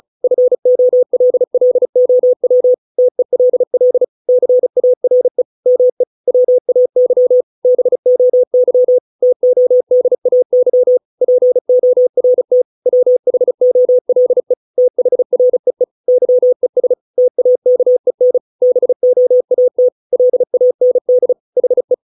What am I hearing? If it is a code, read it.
Morse code, heard as a keyed tone, about 34 wpm, 500 Hz. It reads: FOLLOW TELL CARE ME WAY BOY TODAY PORT WHOLE THREE YES TAKEN BOAT LAND HE
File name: 632 words / 632 Words 35WPM